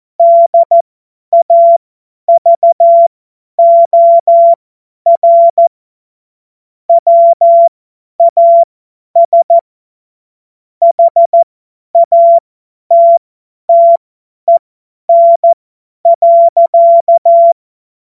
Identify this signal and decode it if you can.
Morse code, heard as a keyed tone, about 14 wpm, spaced as for 10 wpm, 680 Hz. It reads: DAVOR WAS HATTEN.
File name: Morsekurs / Witze / Witz 30. November 2021, 50 F70